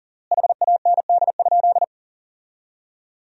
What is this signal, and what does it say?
Morse code, heard as a keyed tone, about 40 wpm, 700 Hz. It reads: HAND?
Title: 3x Word-Spacing - Sentences from Top 200 Words - 40wpm